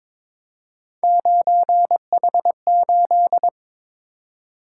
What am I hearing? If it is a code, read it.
Morse code, heard as a keyed tone, about 26 wpm, 700 Hz. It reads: 9H8
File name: Speed-Racing - Three Letter-Number Combinations - 22wpm